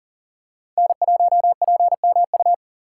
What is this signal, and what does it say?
Morse code, heard as a keyed tone, about 37 wpm, 700 Hz. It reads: N1PMU
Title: USA Call Signs - 2x Character Spacing - 40wpm